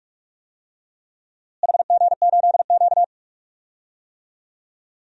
Morse code, heard as a keyed tone, about 42 wpm, 700 Hz. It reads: HG8Q
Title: DXCC-100 Call Signs - 4 Characters - Repeated 2x - 45wpm